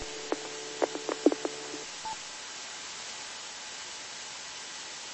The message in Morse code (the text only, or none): none